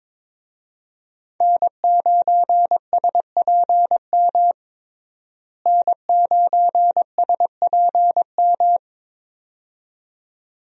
Morse code, heard as a keyed tone, about 22 wpm, 700 Hz. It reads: N9SPM N9SPM
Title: USA Call Signs - 5 Characters - Repeated 2x - 22wpm